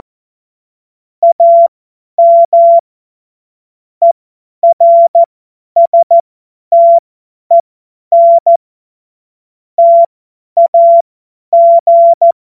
Morse code, heard as a keyed tone, about 14 wpm, spaced as for 10 wpm, 680 Hz. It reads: AM ERSTEN TAG
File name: Morsekurs / Witze / Witz 30. Dezember 2025 50 F70